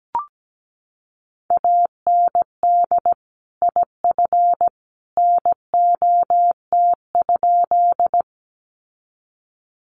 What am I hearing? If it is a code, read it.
Morse code, heard as a keyed tone, about 17 wpm, 700 Hz. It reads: AND IF NOT?
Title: Sentences from Top 600 Words 17wpm